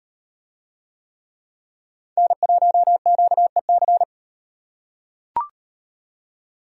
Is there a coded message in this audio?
Yes: Morse code from 2.2 to 4.0 s, heard as a keyed tone, about 34 wpm, 700 Hz. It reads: N1QEC